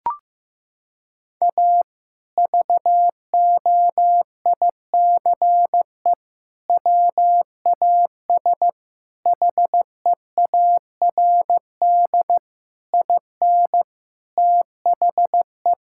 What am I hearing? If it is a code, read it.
Morse code, heard as a keyed tone, about 15 wpm, 700 Hz. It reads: A VOICE WAS HEARD IN THE
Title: Sentences from Top 1000 Words - Review Part 5 - 15wpm